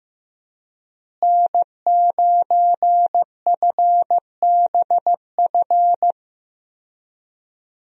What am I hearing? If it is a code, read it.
Morse code, heard as a keyed tone, about 15 wpm, 700 Hz. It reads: N9FBF
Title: USA Call Signs - 5 Characters - Repeated 2x - 15wpm